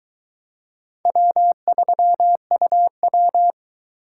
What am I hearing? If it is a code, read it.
Morse code, heard as a keyed tone, about 22 wpm, 700 Hz. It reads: W3UW